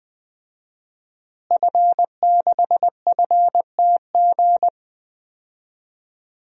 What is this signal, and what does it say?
Morse code, heard as a keyed tone, about 20 wpm, 700 Hz. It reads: F6FTG
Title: DXCC-100 Call Signs - 5 Characters - Repeated 2x - 20wpm